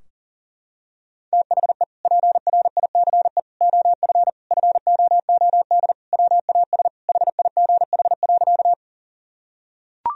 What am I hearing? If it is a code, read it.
Morse code, heard as a keyed tone, about 40 wpm, 700 Hz. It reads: THE PRICE OF FOOD WAS HIGH.
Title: Sentences from Top 1000 Words - Review Part 4 - 40wpm